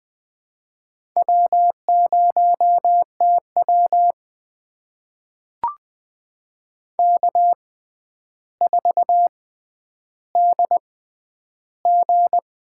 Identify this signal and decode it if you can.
Morse code, heard as a keyed tone, about 21 wpm, spaced as for 10 wpm, 700 Hz. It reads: W0TW K4DG
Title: USA Call Signs - 6x Character Spacing - 20wpm